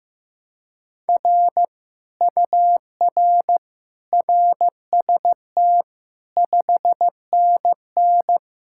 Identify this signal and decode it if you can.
Morse code, heard as a keyed tone, about 15 wpm, 700 Hz. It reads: R UR RST 5NN
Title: Speed-Racing Common QSO Phrases Encore - 15wpm